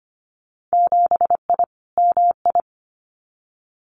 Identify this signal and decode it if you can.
Morse code, heard as a keyed tone, about 25 wpm, 700 Hz. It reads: 7I MI
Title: Winter Field Day Exchanges Encore 25wpm